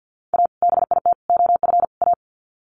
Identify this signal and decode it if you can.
Morse code, heard as a keyed tone, about 50 wpm, 700 Hz. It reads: A BIT OF A
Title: Sets of 4 Words 50wpm